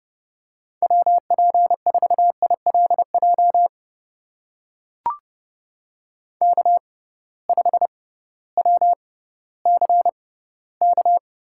Morse code, heard as a keyed tone, about 32 wpm, spaced as for 17 wpm, 700 Hz. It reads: WP4ILJ K5WCK
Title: USA Call Signs - 6x Character Spacing - 30wpm